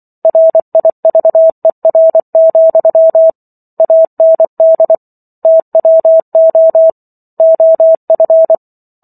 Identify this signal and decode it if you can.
Morse code, heard as a keyed tone, about 24 wpm, 650 Hz. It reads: RIVER, AND TWO OF